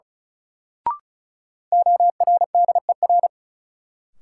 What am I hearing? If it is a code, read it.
Morse code, heard as a keyed tone, about 36 wpm, 700 Hz. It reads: ORDER
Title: Sentences from Top 400 Words - New Word Drill - 35wpm